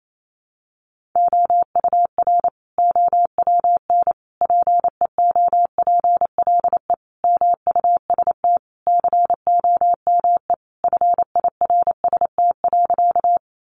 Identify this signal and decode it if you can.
Morse code, heard as a keyed tone, about 28 wpm, 700 Hz. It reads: OUR OWN PEOPLE MUST COME FIRST.
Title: Sentences from Top 300 Words 28wpm